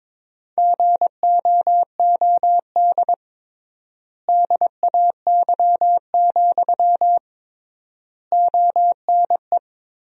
Morse code, heard as a keyed tone, about 22 wpm, 700 Hz. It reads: GOOD DAY, ONE
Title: 3x Word-Spacing - Sentences from Top 200 Words - 22wpm